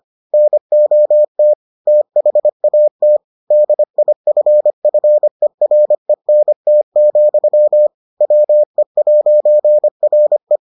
Morse code, heard as a keyed tone, about 25 wpm, 600 Hz. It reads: NOT THAT DIFFERENT, WE'RE